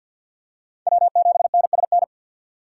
Morse code, heard as a keyed tone, about 51 wpm, 700 Hz. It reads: W7NSN
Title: USA Call Signs - 5 Characters - 6x Character Spacing - 50wpm